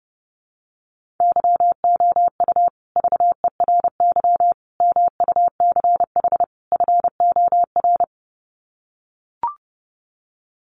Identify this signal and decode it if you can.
Morse code, heard as a keyed tone, about 30 wpm, 700 Hz. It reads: YOU VERY MUCH FOR
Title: Sets of 4 Words 30wpm